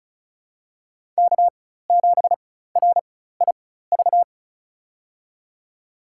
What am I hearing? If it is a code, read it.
Morse code, heard as a keyed tone, about 34 wpm, spaced as for 16 wpm, 700 Hz. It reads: K7RIV
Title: USA Call Signs - 5 Characters - 4x Character Spacing - 35wpm